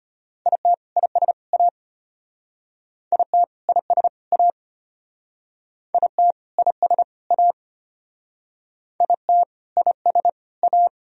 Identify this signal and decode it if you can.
Morse code, heard as a keyed tone, about 33 wpm, 700 Hz. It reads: IT IS A IT IS A IT IS A IT IS A